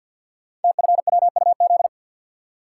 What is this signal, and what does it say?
Morse code, heard as a keyed tone, about 51 wpm, 700 Hz. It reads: TFWUZ
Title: Code Groups - 5 Letter:Numbers - 50wpm